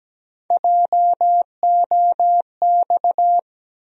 Morse code, heard as a keyed tone, about 17 wpm, 700 Hz. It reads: JOX
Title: USA Call Sign Postfixes - 3 Letters 17wpm